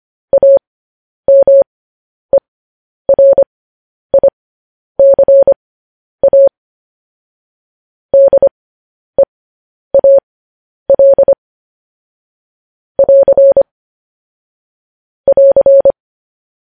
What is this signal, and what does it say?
Morse code, heard as a keyed tone, about 25 wpm, spaced as for 10 wpm, 550 Hz. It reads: AMERICA DEAL <AR> <AR>
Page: News Headlines in Morse Code at 10 WPM